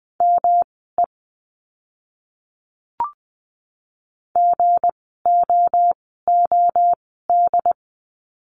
Morse code, heard as a keyed tone, about 20 wpm, spaced as for 14 wpm, 700 Hz. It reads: ME GOOD
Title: 192 Lesson 31f - Letter v - Words 20wpm